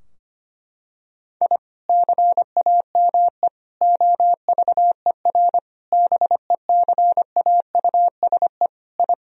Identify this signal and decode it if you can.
Morse code, heard as a keyed tone, about 25 wpm, 700 Hz. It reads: I CAME OVER BECAUSE I
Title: Sentences from Top 1000 Words - Review Part 3 - 25wpm